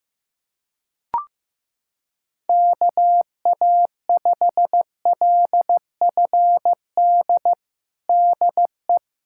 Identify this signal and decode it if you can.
Morse code, heard as a keyed tone, about 15 wpm, 700 Hz. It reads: KA5LFD DE
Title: USA Callsign DE Callsign - 6 Characters - 15wpm